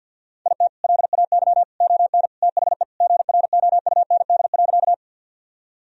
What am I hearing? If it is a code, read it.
Morse code, heard as a keyed tone, about 50 wpm, 700 Hz. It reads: IT LAY ON THE GROUND.
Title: Sentences from Top 1000 Words - Review Part 2 - 50wpm